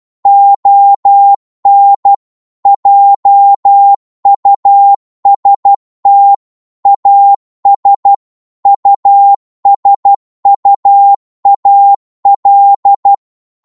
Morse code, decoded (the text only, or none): ON JUST AS USUAL